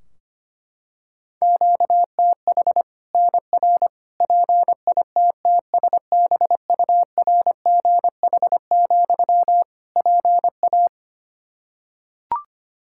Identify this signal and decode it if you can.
Morse code, heard as a keyed tone, about 25 wpm, 700 Hz. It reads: QTH NR PITTSBURGH, PA
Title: Practice QSO - Exchange 1 - Beginning - 25wpm